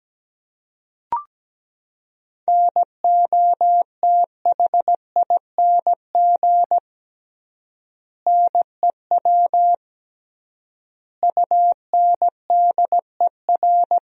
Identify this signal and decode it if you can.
Morse code, heard as a keyed tone, about 17 wpm, 700 Hz. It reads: NOTHING NEW UNDER
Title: 3x Word-Spacing - Sentences from Top 500 Words - 17wpm